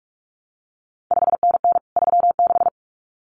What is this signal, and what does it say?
Morse code, heard as a keyed tone, about 46 wpm, 700 Hz. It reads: 5NN 36
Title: CQ WW DX Contest Exchange Encore - 45wpm